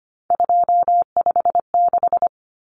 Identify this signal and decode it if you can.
Morse code, heard as a keyed tone, about 25 wpm, 700 Hz. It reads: 256
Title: Common Cut Numbers - 3-Digit 25wpm